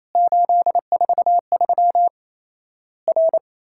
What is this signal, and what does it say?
Morse code, heard as a keyed tone, about 27 wpm, 700 Hz. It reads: 843 R
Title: Worked All Europe DX Contest - QTC Exchange - 28wpm